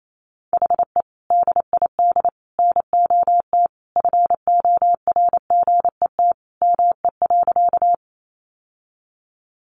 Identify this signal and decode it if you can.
Morse code, heard as a keyed tone, about 28 wpm, 700 Hz. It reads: HE DID NOT FORGET ME.
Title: Sentences from Top 1200 Words 28wpm